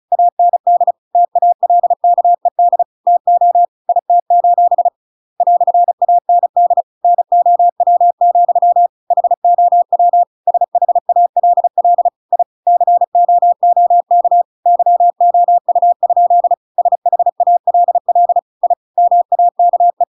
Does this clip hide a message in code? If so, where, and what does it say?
Yes: Morse code from 0.1 to 20.0 s, heard as a keyed tone, about 34 wpm, 700 Hz. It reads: AND TALKED TO IT: "AND NOW, HOW SHALL I COOK YOU? SHALL I MAKE